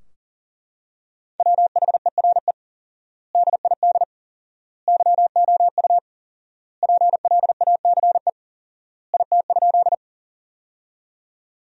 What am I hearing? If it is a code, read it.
Morse code, heard as a keyed tone, about 40 wpm, 700 Hz. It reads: WHERE DID YOU PLACE IT?
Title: 4x Word-Spacing - Sentences from Top 200 Words - 40wpm